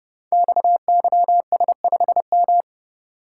Morse code, heard as a keyed tone, about 30 wpm, 700 Hz. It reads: XYS5M
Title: Code Groups - 5 Letter:Numbers - 30wpm